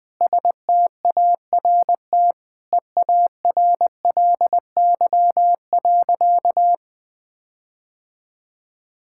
Morse code, heard as a keyed tone, about 21 wpm, 700 Hz. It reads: START EARLY.
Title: Sentences with 7-Word Phrases - 20wpm